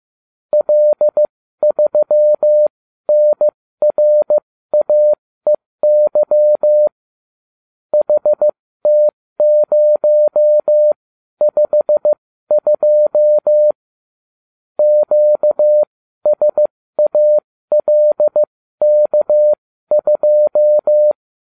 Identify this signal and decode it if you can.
Morse code, heard as a keyed tone, about 15 wpm, spaced as for 13 wpm, 600 Hz. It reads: L3NRAEY HT052 QSALK2